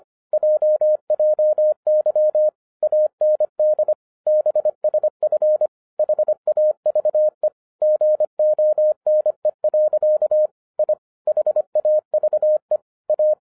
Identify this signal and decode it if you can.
Morse code, heard as a keyed tone, about 25 wpm, 610 Hz. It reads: JJY AND BSF HAVE GONE. I HAVE A